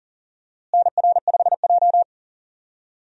Morse code, heard as a keyed tone, about 40 wpm, 700 Hz. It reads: NR5J
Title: USA Call Signs - 4 Characters - Repeated 2x - 40wpm